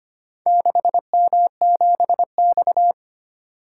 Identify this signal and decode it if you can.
Morse code, heard as a keyed tone, about 25 wpm, 690 Hz. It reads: 6M7X
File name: practice / 25wpm / Four Letter-Number Combinations - 25wpm